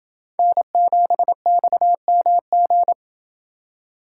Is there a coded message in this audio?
Yes: Morse code from 0.4 to 2.9 s, heard as a keyed tone, about 28 wpm, 700 Hz. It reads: N7XMG